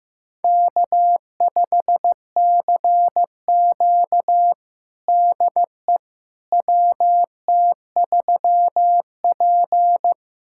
Morse code, heard as a keyed tone, about 15 wpm, 700 Hz. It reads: K5CQ DE WT3P
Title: USA Callsign DE Callsign - 4 Characters - 15wpm